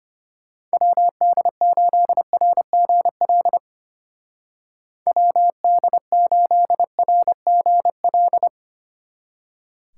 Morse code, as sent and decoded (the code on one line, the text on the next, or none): .-- -.. ---.. .-. --. .-.. / .-- -.. ---.. .-. --. .-..
WD8RGL WD8RGL